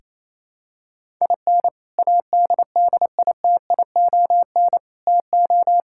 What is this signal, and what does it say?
Morse code, heard as a keyed tone, about 28 wpm, 700 Hz. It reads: IN ADDITION TO
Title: Sentences from Top 1000 Words - Review Part 6 - 28wpm